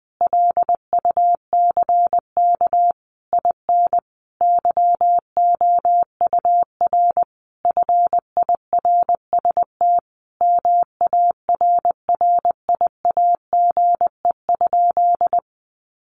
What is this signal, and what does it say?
Morse code, heard as a keyed tone, about 20 wpm, 700 Hz. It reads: LUCK IN YOUR FIRST MARRIAGE?
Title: Sentences from Top 1500 Words 20wpm